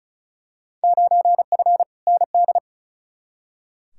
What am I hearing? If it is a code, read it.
Morse code, heard as a keyed tone, about 36 wpm, 700 Hz. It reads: 9F ND
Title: Field Day Exchanges Encore - 35wpm